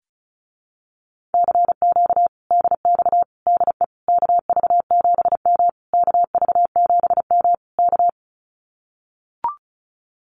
Morse code, heard as a keyed tone, about 34 wpm, 700 Hz. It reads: CQ DX DE KV7M KV7M K
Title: Common QSO Phrases Encore 35wpm